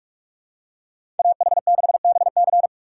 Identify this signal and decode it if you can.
Morse code, heard as a keyed tone, about 46 wpm, 700 Hz. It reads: AH6BC
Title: USA Call Signs - 5 Characters - 6x Character Spacing - 45wpm